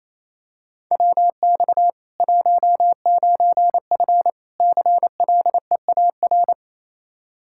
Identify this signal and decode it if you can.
Morse code, heard as a keyed tone, about 28 wpm, 700 Hz. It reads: WX 19F CLEAR
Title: Practice QSO - Exchange 2 - Info - 28wpm